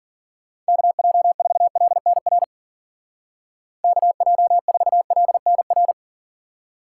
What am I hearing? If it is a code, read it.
Morse code, heard as a keyed tone, about 48 wpm, 700 Hz. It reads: KJ4LNR KJ4LNR
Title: Speed Racing - USA 2x3 Call Signs 40wpm